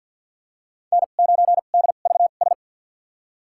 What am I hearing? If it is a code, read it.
Morse code, heard as a keyed tone, about 51 wpm, spaced as for 36 wpm, 700 Hz. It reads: N9DVS
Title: USA Call Signs - 5 Characters - 2x Character Spacing - 50wpm